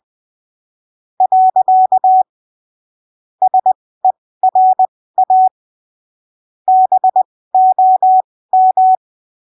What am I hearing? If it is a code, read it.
Morse code, heard as a keyed tone, about 20 wpm, spaced as for 15 wpm, 750 Hz. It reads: . SERA BOM
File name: arquivos / texto / texto 25 20ppm